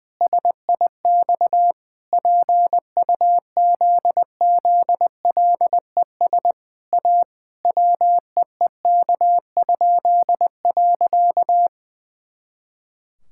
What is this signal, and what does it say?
Morse code, heard as a keyed tone, about 20 wpm, 700 Hz. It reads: SIX PUZZLES A WEEK?.
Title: Pangram Challenge v2 - 20wpm